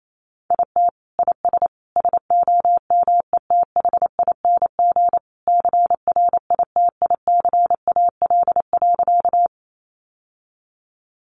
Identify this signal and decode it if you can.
Morse code, heard as a keyed tone, about 28 wpm, 700 Hz. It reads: IT IS SOMETHING CRITICAL.
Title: Sentences from Top 1100 Words 28wpm